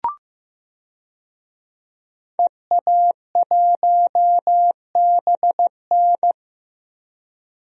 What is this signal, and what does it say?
Morse code, heard as a keyed tone, about 15 wpm, 700 Hz. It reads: EA1BN
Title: DXCC-100 Call Signs - 5 Characters - Repeated 2x - 15wpm